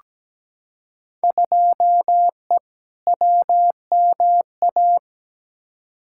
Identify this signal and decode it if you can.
Morse code, heard as a keyed tone, about 17 wpm, 700 Hz. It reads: 2E WMA
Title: Field Day Exchanges Encore - 17wpm